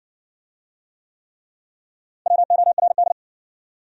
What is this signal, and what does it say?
Morse code, heard as a keyed tone, about 60 wpm, 700 Hz. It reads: WORD